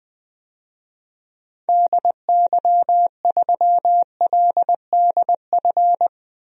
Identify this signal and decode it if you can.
Morse code, heard as a keyed tone, about 21 wpm, 700 Hz. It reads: DY3LDF
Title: DXCC-100 Call Signs - 6 Characters - 20wpm